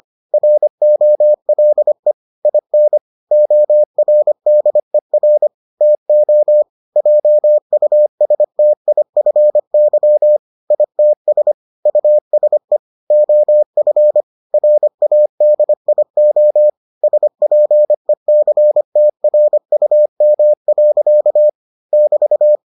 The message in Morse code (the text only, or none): ROLE IN ORDER TO JUSTIFY ITS USE OF RADIO SPECTRUM. <BT>